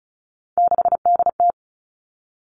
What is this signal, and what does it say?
Morse code, heard as a keyed tone, about 34 wpm, 700 Hz. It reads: 6DT
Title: All Cut Numbers - 3-Digit 35wpm